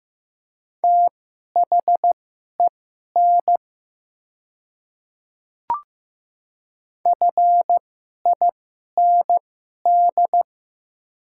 Morse code, heard as a keyed tone, about 15 wpm, spaced as for 11 wpm, 700 Hz. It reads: THEN FIND
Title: 2x Character Spacing - Top 200 Words - 15wpm